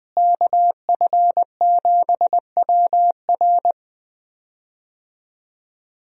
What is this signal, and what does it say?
Morse code, heard as a keyed tone, about 20 wpm, 700 Hz. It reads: KF7WR
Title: USA Call Signs - 5 Characters - Repeated 3x - 20wpm